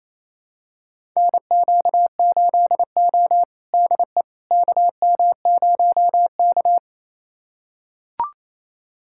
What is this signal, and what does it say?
Morse code, heard as a keyed tone, about 28 wpm, 700 Hz. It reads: NQ8O DE KM0K
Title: USA Callsign DE Callsign - 4 Characters - 28wpm